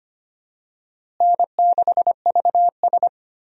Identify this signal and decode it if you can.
Morse code, heard as a keyed tone, about 25 wpm, 700 Hz. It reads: N6VS
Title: USA Call Signs - 4 Characters - Repeated 3x - 25wpm